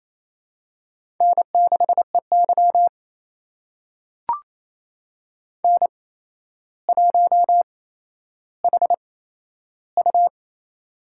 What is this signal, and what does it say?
Morse code, heard as a keyed tone, about 28 wpm, spaced as for 12 wpm, 700 Hz. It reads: N6EY N1HU
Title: USA Call Signs - 4 Characters - 8x Character Spacing - 28wpm